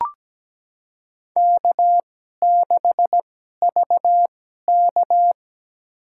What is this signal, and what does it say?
Morse code, heard as a keyed tone, about 17 wpm, spaced as for 12 wpm, 700 Hz. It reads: K6VK
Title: USA Call Signs - 4 Characters - 2x Character Spacing - 17wpm